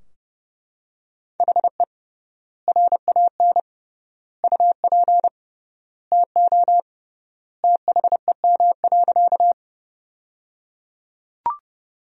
Morse code, heard as a keyed tone, about 30 wpm, 700 Hz. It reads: HE RAN UP TO THEM.
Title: 3x Word-Spacing - Sentences from Top 500 Words - 30wpm